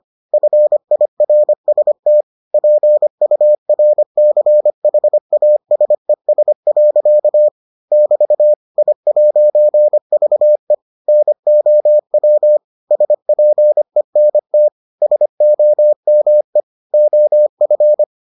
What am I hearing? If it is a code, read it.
Morse code, heard as a keyed tone, about 25 wpm, 600 Hz. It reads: FIRST PURCHASES. <BT> I'VE NOW SPENT SOME OF